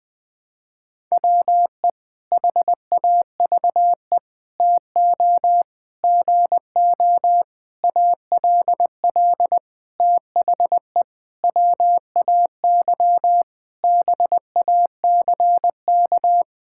Morse code, heard as a keyed tone, about 20 wpm, 700 Hz. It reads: WE HAVE TO GO ALL THE WAY BACK
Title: Sentences with 8-Word Phrases - 20wpm